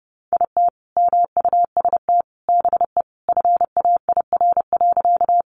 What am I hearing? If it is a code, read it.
Morse code, heard as a keyed tone, about 30 wpm, 700 Hz. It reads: IT MUST BE FAIR.
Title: Sentences from Top 1300 Words 30wpm